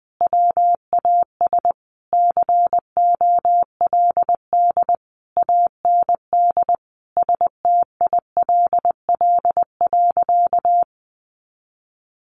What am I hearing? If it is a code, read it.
Morse code, heard as a keyed tone, about 19 wpm, 700 Hz. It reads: WAS COLD AND STILL.
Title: Sentences from Top 400 Words 20wpm